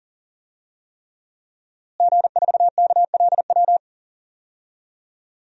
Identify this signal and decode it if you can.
Morse code, heard as a keyed tone, about 37 wpm, 700 Hz. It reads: G4KLW
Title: DXCC-100 Call Signs - 5 Characters - Repeated 3x - 40wpm